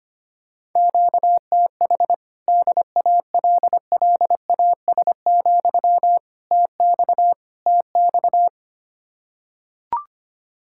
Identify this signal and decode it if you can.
Morse code, heard as a keyed tone, about 25 wpm, 700 Hz. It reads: QTH DALLAS, TX TX
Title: Practice QSO - Exchange 1 - Beginning - 25wpm